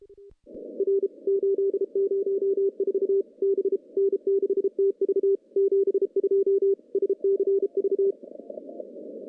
Morse code, heard as a keyed tone, about 34 wpm, 390 Hz. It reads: R 804 B N6TV 72 SCV
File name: CW-SS-k5tr